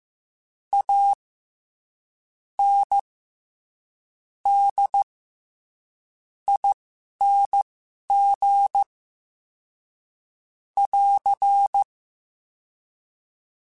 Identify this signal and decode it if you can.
Morse code, heard as a keyed tone, about 15 wpm, spaced as for 11 wpm, 800 Hz. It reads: A N D ING <AR>
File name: Special INGs-mp3s / Special INGs-mp3s / SlowToFaster ing @10